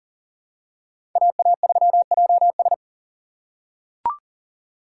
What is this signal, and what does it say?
Morse code, heard as a keyed tone, about 40 wpm, 700 Hz. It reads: AA3JS
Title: USA Call Signs - 5 Characters - 2x Character Spacing - 40wpm